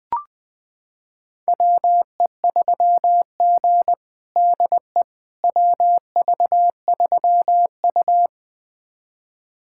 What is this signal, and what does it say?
Morse code, heard as a keyed tone, about 20 wpm, 700 Hz. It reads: WE3G DE WV3U
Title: USA Callsign DE Callsign - 4 Characters - 20wpm